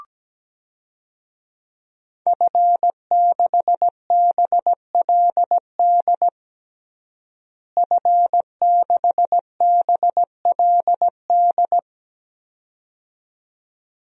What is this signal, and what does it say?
Morse code, heard as a keyed tone, about 17 wpm, 700 Hz. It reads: F6BLD F6BLD
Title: DXCC-100 Call Signs - 5 Characters - Repeated 2x - 17wpm